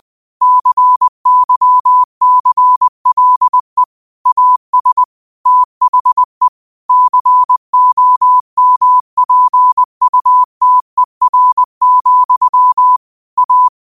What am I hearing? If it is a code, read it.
Morse code, heard as a keyed tone, about 20 wpm, 1000 Hz. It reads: CYCLE AS THE COMPUTER, A
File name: qotdfiles / 20wpm